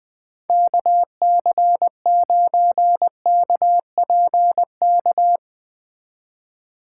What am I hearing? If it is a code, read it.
Morse code, heard as a keyed tone, about 20 wpm, 700 Hz. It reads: KC9KPK
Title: Speed Racing - USA Call Signs 17wpm